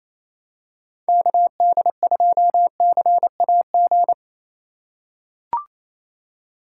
Morse code, heard as a keyed tone, about 28 wpm, 700 Hz. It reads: KD2CAG
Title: USA Call Signs - 6 Characters - 3x Character Spacing - 28wpm